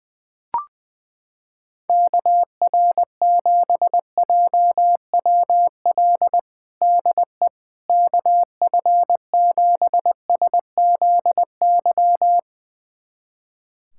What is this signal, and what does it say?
Morse code, heard as a keyed tone, about 21 wpm, 700 Hz. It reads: KR7JWL DE KF7SZY
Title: USA Callsign DE Callsign - 6 Characters - 20wpm